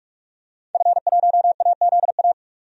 Morse code, heard as a keyed tone, about 43 wpm, 700 Hz. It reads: F1AZA
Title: DXCC-100 Call Signs - 5 Characters - 45wpm